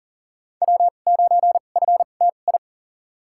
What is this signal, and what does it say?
Morse code, heard as a keyed tone, about 40 wpm, spaced as for 29 wpm, 700 Hz. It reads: W9FTI
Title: USA Call Signs - 2x Character Spacing - 40wpm